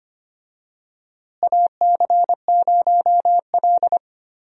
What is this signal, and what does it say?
Morse code, heard as a keyed tone, about 25 wpm, 700 Hz. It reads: AC0L
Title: USA Call Signs - 4 Characters - Repeated 3x - 25wpm